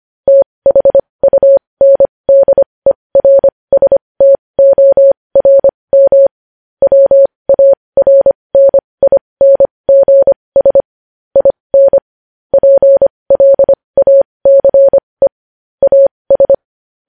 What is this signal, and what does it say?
Morse code, heard as a keyed tone, about 25 wpm, spaced as for 20 wpm, 550 Hz. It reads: THUNDERSTORM WARNINGS IN PLACE AS
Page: news-headlines-in-morse-code-at-20-wpm